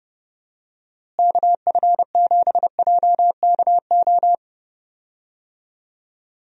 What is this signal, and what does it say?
Morse code, heard as a keyed tone, about 30 wpm, 700 Hz. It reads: KF7JKO
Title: USA Call Signs - 6 Characters - Repeated 2x - 30wpm